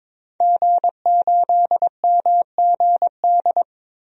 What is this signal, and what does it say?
Morse code, heard as a keyed tone, about 22 wpm, 700 Hz. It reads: G8MGD
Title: DXCC-100 Call Signs - 5 Characters - Repeated 2x - 22wpm